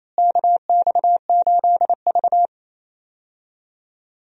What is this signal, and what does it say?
Morse code, heard as a keyed tone, about 28 wpm, 700 Hz. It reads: KX8V
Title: USA Call Signs - 4 Characters - Repeated 2x - 28wpm